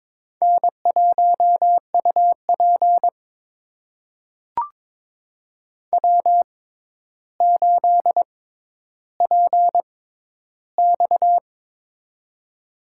Morse code, heard as a keyed tone, about 22 wpm, spaced as for 11 wpm, 700 Hz. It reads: N1UP W8PX
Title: USA Call Signs - 4 Characters - 6x Character Spacing - 22wpm